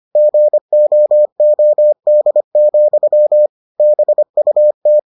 Morse code, heard as a keyed tone, about 25 wpm, 600 Hz. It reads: GOOD, BUT